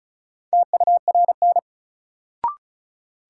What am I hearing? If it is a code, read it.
Morse code, heard as a keyed tone, about 36 wpm, 700 Hz. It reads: TURN
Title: Sets of 4 Words - New Word Drill - 35wpm